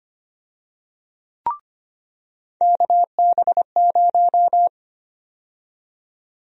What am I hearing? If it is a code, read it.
Morse code, heard as a keyed tone, about 25 wpm, 700 Hz. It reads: KB0
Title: USA Call Sign Prefixes - 25wpm